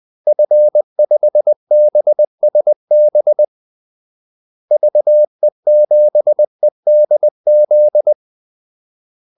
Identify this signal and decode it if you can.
Morse code, heard as a keyed tone, about 20 wpm, 600 Hz. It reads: F5BSB VE7EDZ
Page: Kaldesignaler 20 wpm